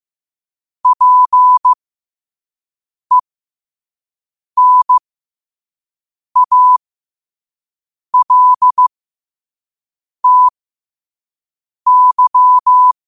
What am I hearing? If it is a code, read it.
Morse code, heard as a keyed tone, about 15 wpm, spaced as for 5 wpm, 1000 Hz. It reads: PENALTY
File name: qotdfiles / 5wpm